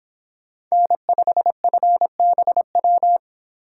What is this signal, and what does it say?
Morse code, heard as a keyed tone, about 29 wpm, 700 Hz. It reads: N5FBW